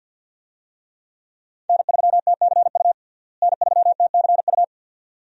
Morse code, heard as a keyed tone, about 51 wpm, 700 Hz. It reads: N3TCU N3TCU
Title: USA Call Signs - 5 Characters - Repeated 2x - 50wpm